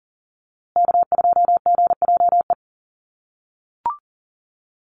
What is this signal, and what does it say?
Morse code, heard as a keyed tone, about 40 wpm, 700 Hz. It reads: K2GJE
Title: USA Call Signs Repeated 2x 40wpm